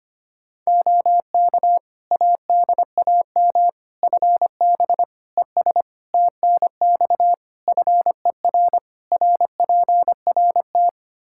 Morse code, heard as a keyed tone, about 25 wpm, 700 Hz. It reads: OK ADAM FB ES TNX FER RPRT
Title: Practice QSO - Exchange 2 - Info - 25wpm